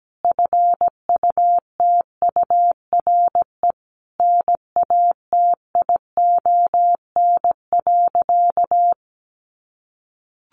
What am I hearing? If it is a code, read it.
Morse code, heard as a keyed tone, about 17 wpm, 700 Hz. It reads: FUTURE NATION.
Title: A Sound Of Thunder 17wpm